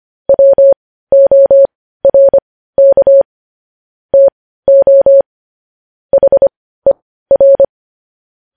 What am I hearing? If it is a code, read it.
Morse code, heard as a keyed tone, about 25 wpm, spaced as for 15 wpm, 550 Hz. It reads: WORK TO HER
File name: podcast / News headlines-1x-25x15